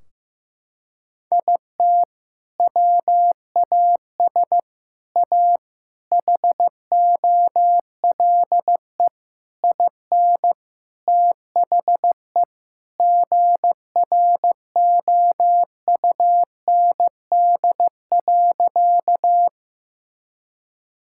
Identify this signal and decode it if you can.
Morse code, heard as a keyed tone, about 15 wpm, 700 Hz. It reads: IT WAS A HOLE IN THE GROUND.
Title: Sentences from Top 1600 Words 15wpm